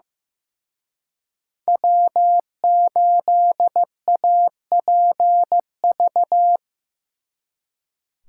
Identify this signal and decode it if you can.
Morse code, heard as a keyed tone, about 18 wpm, 700 Hz. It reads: W8APV
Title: Speed Racing - USA 1x3 Call Signs 15wpm